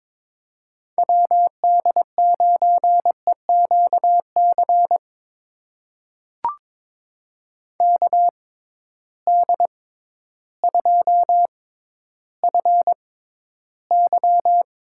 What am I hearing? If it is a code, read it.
Morse code, heard as a keyed tone, about 22 wpm, spaced as for 12 wpm, 700 Hz. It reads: WD9EQC KD2FY
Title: USA Call Signs - 6 Characters - 6x Character Spacing - 22wpm